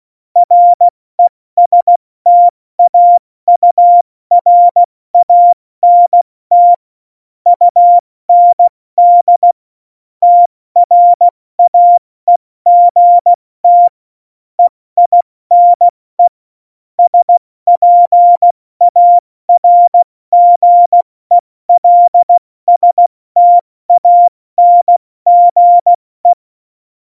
Morse code, heard as a keyed tone, about 16 wpm, 700 Hz. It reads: RESTAURANT UND TRAEGT EINE SPARGELSTANGE
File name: Morsekurs / Witze / Witz 16. April 2019, 70 F80